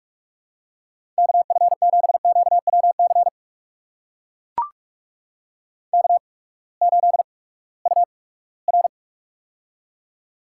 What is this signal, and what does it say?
Morse code, heard as a keyed tone, about 42 wpm, spaced as for 22 wpm, 700 Hz. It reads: KF7QWC K8UR
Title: USA Call Signs - 8x Character Spacing - 45wpm